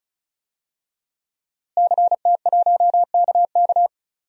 Morse code, heard as a keyed tone, about 36 wpm, 700 Hz. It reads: CT1KK
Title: DXCC-100 Call Signs - 5 Characters - Repeated 3x - 35wpm